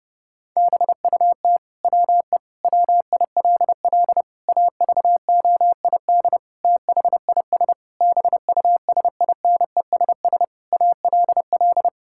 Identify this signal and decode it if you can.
Morse code, heard as a keyed tone, about 30 wpm, 700 Hz. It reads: BUT WE WILL AVOID THIS BUSINESS ALL
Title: Sentences from Top 1000 Words - Review Part 6 - 30wpm